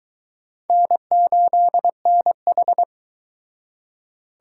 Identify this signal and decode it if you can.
Morse code, heard as a keyed tone, about 22 wpm, 700 Hz. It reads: N8NH